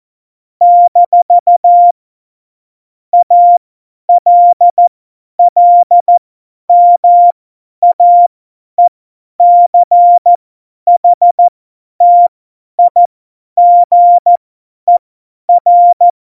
Morse code, heard as a keyed tone, about 14 wpm, spaced as for 10 wpm, 700 Hz. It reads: - ALLMAECHTIGER
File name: Morsekurs / Witze / Witz 27. Oktober 2020, 50 F70